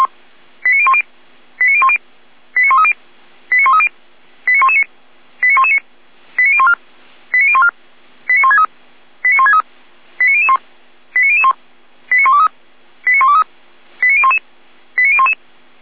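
Five-tone selective call (ZVEI), heard as quick runs of five tones, tones of about 70 ms. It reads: 89010 89010 89130 89130 89119 89119 89014 89014 89163 89163 89001 89001 89123 89123 89011 89011
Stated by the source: ZVEI five-tone selective call received by SDR.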